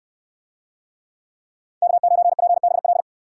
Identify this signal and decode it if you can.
Morse code, heard as a keyed tone, about 69 wpm, 700 Hz. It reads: K9PBL